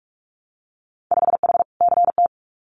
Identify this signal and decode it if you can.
Morse code, heard as a keyed tone, about 46 wpm, 690 Hz. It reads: 5H CT
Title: Winter Field Day Exchanges Encore 45wpm